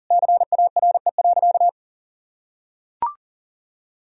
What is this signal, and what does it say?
Morse code, heard as a keyed tone, about 37 wpm, 700 Hz. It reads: CARE.
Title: Sentences from Top 1000 Words - Review Part 3 - 40wpm